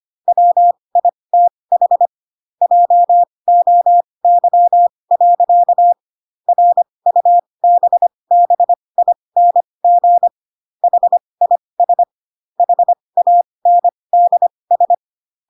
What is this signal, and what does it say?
Morse code, heard as a keyed tone, about 25 wpm, spaced as for 20 wpm, 700 Hz. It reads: WITH JOY. RUBBING HIS HANDS
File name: Pinocchio / Pinocchio ch1 15 25 ik1hge